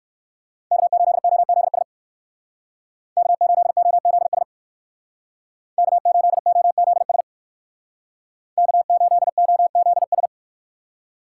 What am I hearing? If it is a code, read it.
Morse code, heard as a keyed tone, about 59 wpm, 700 Hz. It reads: K8OZS K8OZS K8OZS K8OZS